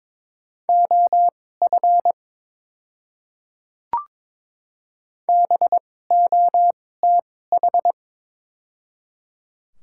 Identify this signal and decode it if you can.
Morse code, heard as a keyed tone, about 22 wpm, spaced as for 16 wpm, 700 Hz. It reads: OF BOTH
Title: 2x Character Spacing - Top 500 Words - 22wpm